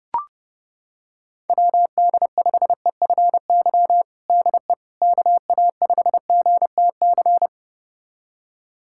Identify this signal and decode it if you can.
Morse code, heard as a keyed tone, about 32 wpm, 690 Hz. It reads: WD5EFY DE KA5GTC
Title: USA Callsign DE Callsign - 6 Characters - 30wpm